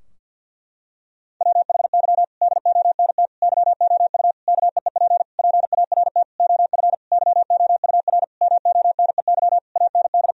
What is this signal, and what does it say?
Morse code, heard as a keyed tone, about 52 wpm, 700 Hz. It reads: WHY DONT YOU KEEP PART OF YOUR MONEY AND
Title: Sentences from Top 1000 Words - Review Part 4 - 50wpm